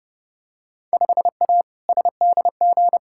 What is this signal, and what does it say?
Morse code, heard as a keyed tone, about 30 wpm, 700 Hz. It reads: 5A SDG
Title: Field Day Exchanges Encore - 30wpm